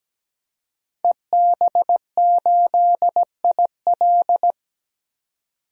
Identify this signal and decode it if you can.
Morse code, heard as a keyed tone, about 17 wpm, 700 Hz. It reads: EB8IL
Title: DXCC-100 Call Signs - 5 Characters - 17wpm